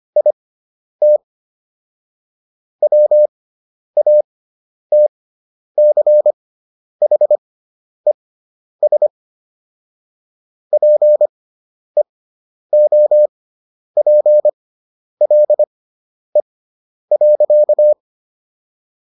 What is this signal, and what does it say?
Morse code, heard as a keyed tone, about 25 wpm, spaced as for 10 wpm, 600 Hz. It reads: IT WATCHES PEOPLE.